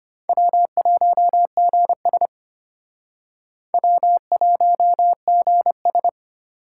Morse code, heard as a keyed tone, about 34 wpm, 700 Hz. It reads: W1GS W1GS